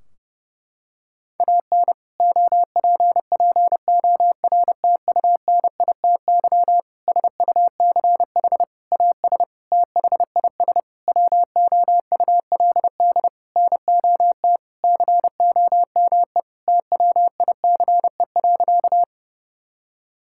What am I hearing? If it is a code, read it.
Morse code, heard as a keyed tone, about 30 wpm, 700 Hz. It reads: AN OPPORTUNITY SUCH AS THIS WOULD NOT COME TWICE.
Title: Sentences from Top 1600 Words 30wpm